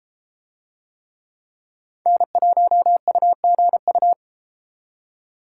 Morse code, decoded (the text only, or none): N1UGU